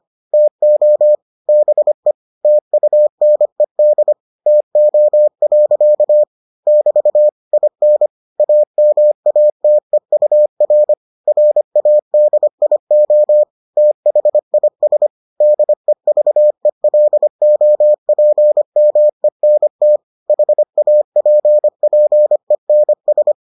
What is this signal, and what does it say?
Morse code, heard as a keyed tone, about 25 wpm, 600 Hz. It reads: TO BE TUNED TO. <BT> IN AMATEUR RADIO THIS DEVELOPMENT HAPPENS